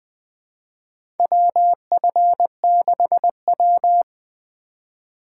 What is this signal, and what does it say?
Morse code, heard as a keyed tone, about 20 wpm, 700 Hz. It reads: WF6W
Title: USA Call Signs - 4 Characters - 3x Character Spacing - 20wpm